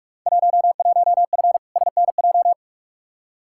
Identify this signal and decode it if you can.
Morse code, heard as a keyed tone, about 47 wpm, 700 Hz. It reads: 11F SNJ
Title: Field Day Exchanges Encore - 45wpm